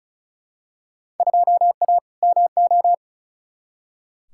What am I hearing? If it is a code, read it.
Morse code, heard as a keyed tone, about 36 wpm, 700 Hz. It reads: 2A MO
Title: Field Day Exchanges Encore - 35wpm